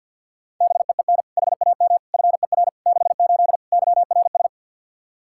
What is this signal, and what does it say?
Morse code, heard as a keyed tone, about 51 wpm, 700 Hz. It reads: BEEN HAM FER 68 YRS
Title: Practice QSO - Exchange 3 - More Info - 50wpm